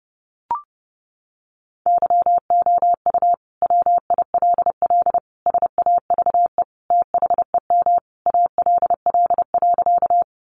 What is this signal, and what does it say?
Morse code, heard as a keyed tone, about 30 wpm, 700 Hz. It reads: YOU WILL SAVE THEM ALL.
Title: Sentences from Top 1100 Words 30wpm